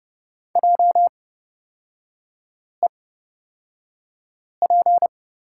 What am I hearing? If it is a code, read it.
Morse code, heard as a keyed tone, about 30 wpm, spaced as for 5 wpm, 690 Hz. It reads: JEP
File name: Lesson 38e - Letter q - Three Characters 30wpm